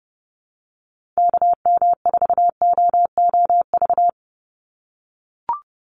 Morse code, heard as a keyed tone, about 30 wpm, 700 Hz. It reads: KM4OOV
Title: USA Call Signs 30wpm